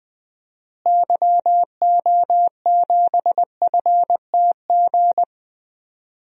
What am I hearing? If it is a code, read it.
Morse code, heard as a keyed tone, about 20 wpm, 700 Hz. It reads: YO7FTG
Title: DXCC-100 Call Signs - 6 Characters - Repeated 3x - 20wpm